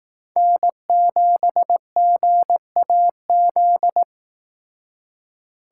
Morse code, heard as a keyed tone, about 18 wpm, 700 Hz. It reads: N7GAZ
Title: Speed Racing - USA Call Signs 15wpm